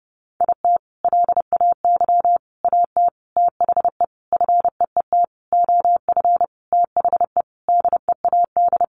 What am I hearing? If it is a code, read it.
Morse code, heard as a keyed tone, about 30 wpm, 700 Hz. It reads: IT LAY AT THE FEET OF THE DEAD
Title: Sentences from Top 1500 Words 30wpm